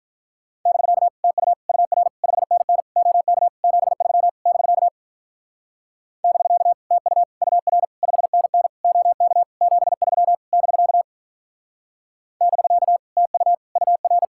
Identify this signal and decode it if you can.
Morse code, heard as a keyed tone, about 45 wpm, 700 Hz. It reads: <BK> TU UR 5NN OK 73 <BK> <BK> TU UR 5NN OK 73 <BK> <BK> TU UR